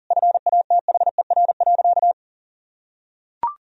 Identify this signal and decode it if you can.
Morse code, heard as a keyed tone, about 40 wpm, 700 Hz. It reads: FATHER.
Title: Sentences from Top 1000 Words - Review Part 2 - 40wpm